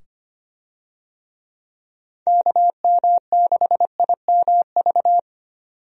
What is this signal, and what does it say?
Morse code, heard as a keyed tone, about 25 wpm, 700 Hz. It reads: KM6IMV
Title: USA Call Signs - 6 Characters - 25wpm